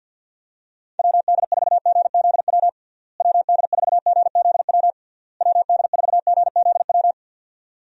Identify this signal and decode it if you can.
Morse code, heard as a keyed tone, about 51 wpm, 700 Hz. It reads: WD4GZW WD4GZW WD4GZW
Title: USA Call Signs - 6 Characters - Repeated 3x - 50wpm